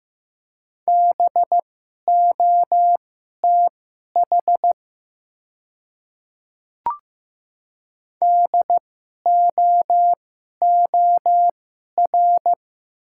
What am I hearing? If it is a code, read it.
Morse code, heard as a keyed tone, about 15 wpm, spaced as for 11 wpm, 700 Hz. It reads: BOTH DOOR
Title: 2x Character Spacing - Top 300 Words - 15wpm